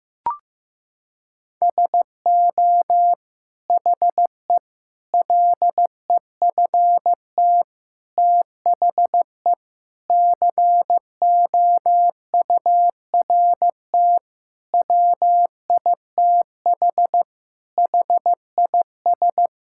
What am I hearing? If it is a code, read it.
Morse code, heard as a keyed tone, about 16 wpm, 700 Hz. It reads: SO HE LEFT THE COURT WITH HIS
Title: Sentences from Top 1600 Words 15wpm